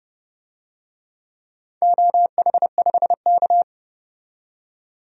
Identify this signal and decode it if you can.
Morse code, heard as a keyed tone, about 30 wpm, 700 Hz. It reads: OH5K
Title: DXCC-100 Call Signs - 4 Characters - 30wpm